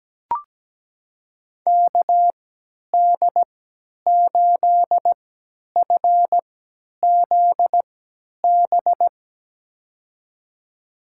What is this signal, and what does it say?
Morse code, heard as a keyed tone, about 17 wpm, spaced as for 10 wpm, 700 Hz. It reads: KD8FZB
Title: USA Call Signs - 6 Characters - 3x Character Spacing - 17wpm